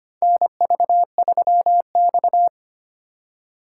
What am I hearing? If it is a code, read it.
Morse code, heard as a keyed tone, about 25 wpm, 700 Hz. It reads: NV3X
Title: USA Call Signs - 4 Characters - 3x Character Spacing - 25wpm